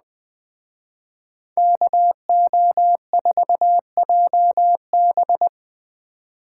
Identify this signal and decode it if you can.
Morse code, heard as a keyed tone, about 20 wpm, 700 Hz. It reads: KO4JB